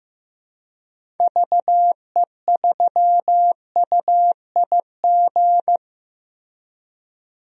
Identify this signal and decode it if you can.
Morse code, heard as a keyed tone, about 15 wpm, 700 Hz. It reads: VE3UIG
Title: DXCC-100 Call Signs - 6 Characters - 15wpm